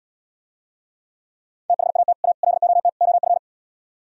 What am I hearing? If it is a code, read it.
Morse code, heard as a keyed tone, about 77 wpm, 700 Hz. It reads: THAT A LOT OF